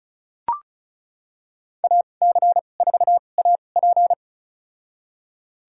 Morse code, heard as a keyed tone, about 34 wpm, spaced as for 25 wpm, 700 Hz. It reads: AC4AP
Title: USA Call Signs - 5 Characters - 2x Character Spacing - 35wpm